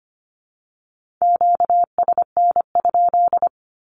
Morse code, heard as a keyed tone, about 25 wpm, 700 Hz. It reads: QSN?
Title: Q-Codes with Spoken Meaning - 25wpm